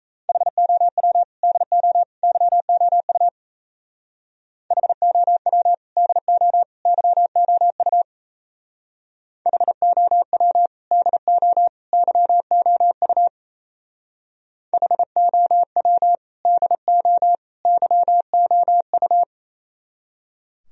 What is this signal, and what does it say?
Morse code, heard as a keyed tone, about 36 wpm, 700 Hz. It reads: HOW DO YOU HOW DO YOU HOW DO YOU HOW DO YOU